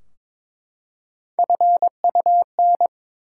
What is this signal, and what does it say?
Morse code, heard as a keyed tone, about 22 wpm, 700 Hz. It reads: FUN
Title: Three Letter Words - 22wpm